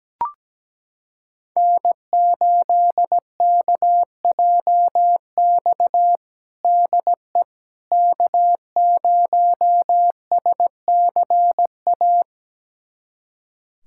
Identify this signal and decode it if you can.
Morse code, heard as a keyed tone, about 17 wpm, 700 Hz. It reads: N8KJX DE K0SCA
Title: USA Callsign DE Callsign - 5 Characters - 17wpm